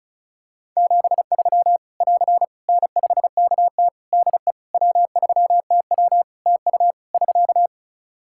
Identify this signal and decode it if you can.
Morse code, heard as a keyed tone, about 34 wpm, 700 Hz. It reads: 73 <AR> N5KT DE W3TW TU <SK>
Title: Practice QSO - Exchange 4 - Ending - 35wpm